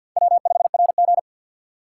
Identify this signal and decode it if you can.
Morse code, heard as a keyed tone, about 50 wpm, 700 Hz. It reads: W5RG
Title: USA Call Signs - 2x Character Spacing - 50wpm